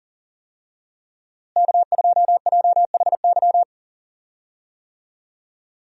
Morse code, heard as a keyed tone, about 40 wpm, 700 Hz. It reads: K2JHY
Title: USA Call Signs - 5 Characters - Repeated 2x - 40wpm